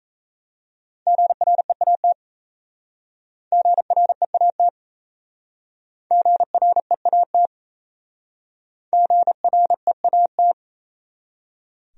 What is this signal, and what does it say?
Morse code, heard as a keyed tone, about 36 wpm, 700 Hz. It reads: GREAT GREAT GREAT GREAT